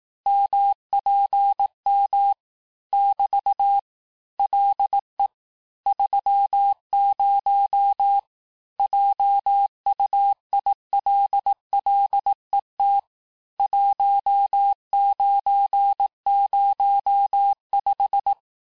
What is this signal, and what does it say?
Morse code, heard as a keyed tone, about 18 wpm, 800 Hz. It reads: MPM <BT> LE 30 JUILLET 1905
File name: texte-clair-vitesse-17mpm-puis-22mpm